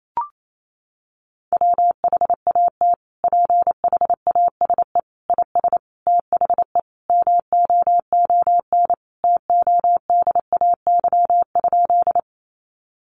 Morse code, heard as a keyed tone, about 28 wpm, 700 Hz. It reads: WHAT PHASE IS THE MOON TODAY?
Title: Sentences from Top 1200 Words 28wpm